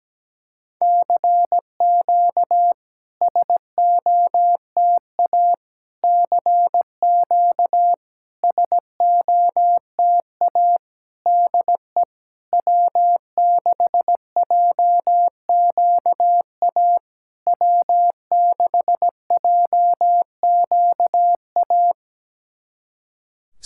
Morse code, heard as a keyed tone, about 17 wpm, 700 Hz. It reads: CQ SOTA CQ SOTA DE W6JQA W6JQA
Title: SOTA QSO Practice 17wpm